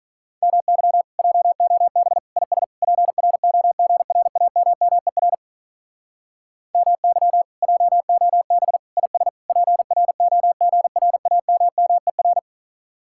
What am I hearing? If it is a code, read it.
Morse code, heard as a keyed tone, about 43 wpm, 700 Hz. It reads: MY JOB IS PROGRAMMER MY JOB IS PROGRAMMER
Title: Speed-Racing Common QSO Phrases Encore - 35wpm